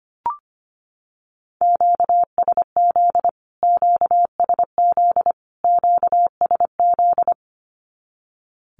Morse code, heard as a keyed tone, about 25 wpm, 700 Hz. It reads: QSZ QSZ QSZ
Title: Q-Codes Repeated 3x 25wpm